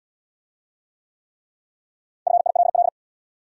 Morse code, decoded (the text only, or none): FILL